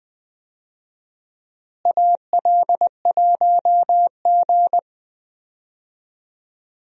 Morse code, heard as a keyed tone, about 20 wpm, 700 Hz. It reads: AL1G
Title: USA Call Signs - 4 Characters - Repeated 2x - 20wpm